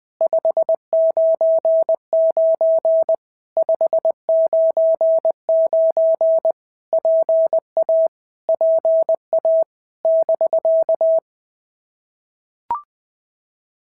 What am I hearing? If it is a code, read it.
Morse code, heard as a keyed tone, about 20 wpm, 650 Hz. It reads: 599 599 PA PA <BK>
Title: POTA QSO Practice - 20wpm